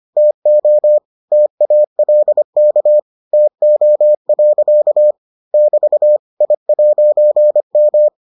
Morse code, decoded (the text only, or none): TO TALK TO. <BT> I'M